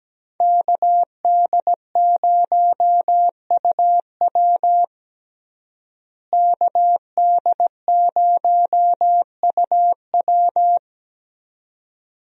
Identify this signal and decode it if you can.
Morse code, heard as a keyed tone, about 17 wpm, 700 Hz. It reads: KD0UW KD0UW
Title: USA Call Signs - 5 Characters - Repeated 2x - 17wpm